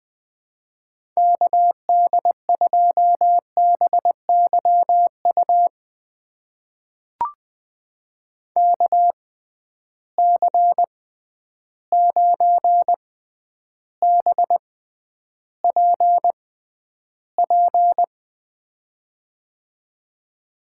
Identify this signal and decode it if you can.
Morse code, heard as a keyed tone, about 20 wpm, spaced as for 10 wpm, 700 Hz. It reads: KD2BYU KC9BPP
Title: USA Call Signs - 6 Characters - 6x Character Spacing - 20wpm